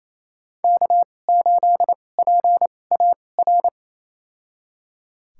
Morse code, heard as a keyed tone, about 28 wpm, spaced as for 20 wpm, 700 Hz. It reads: K8PAR
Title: USA Call Signs - 5 Characters - 2x Character Spacing - 28wpm